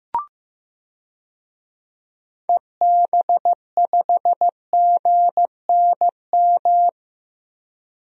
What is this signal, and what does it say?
Morse code, heard as a keyed tone, about 15 wpm, 700 Hz. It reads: EB5GNM
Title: DXCC-100 Call Signs - 6 Characters - 15wpm